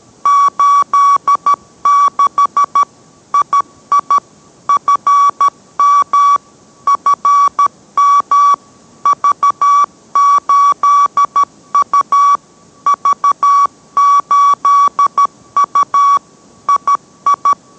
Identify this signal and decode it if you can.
Morse code, heard as a keyed tone, about 13 wpm, 1140 Hz. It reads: 86 II FM FM V8U V8U II